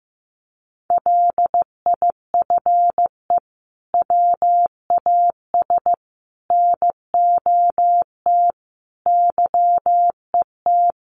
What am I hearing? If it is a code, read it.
Morse code, heard as a keyed tone, about 15 wpm, 700 Hz. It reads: LIFE WAS NOT YET
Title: Sentences from Top 1200 Words 15wpm